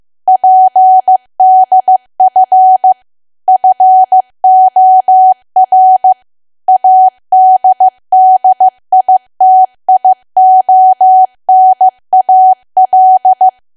Morse code, heard as a keyed tone, about 15 wpm, 750 Hz. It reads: PDF FOR ADDITIONAL
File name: cwa / W1AW-Level2 / 15WPM